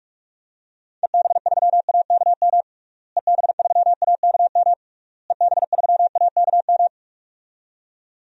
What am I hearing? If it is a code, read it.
Morse code, heard as a keyed tone, about 44 wpm, 700 Hz. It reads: EB3AKM EB3AKM EB3AKM
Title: DXCC-100 Call Signs - 6 Characters - Repeated 3x - 45wpm